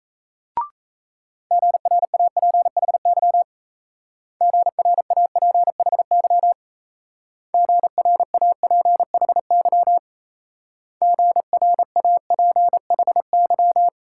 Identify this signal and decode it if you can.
Morse code, heard as a keyed tone, about 35 wpm, 700 Hz. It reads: GRAPHY GRAPHY GRAPHY GRAPHY
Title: Speed-Racing Multi-letter Phonemes - 28wpm